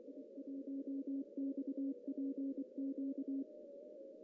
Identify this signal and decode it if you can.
Morse code, heard as a keyed tone, about 23 wpm, 270 Hz. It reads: I1XPQ
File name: CW / cqww-cw-k5tr